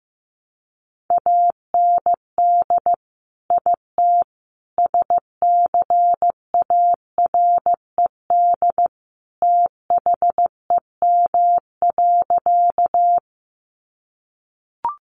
Morse code, heard as a keyed tone, about 15 wpm, 700 Hz. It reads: AND IT SCARED THEM.
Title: Sentences from Top 700 Words 15wpm